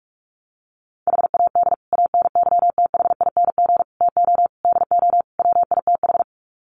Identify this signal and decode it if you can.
Morse code, heard as a keyed tone, about 44 wpm, 700 Hz. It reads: HAD ANYTHING TO DO WITH
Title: Sets of 5 Words Encore 45wpm